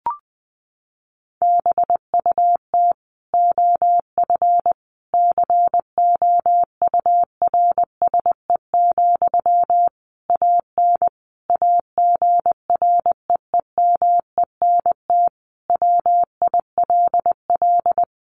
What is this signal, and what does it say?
Morse code, heard as a keyed tone, about 19 wpm, 700 Hz. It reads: BUT OF COURSE, AN AGREEMENT WILL
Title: Sentences from Top 1500 Words 20wpm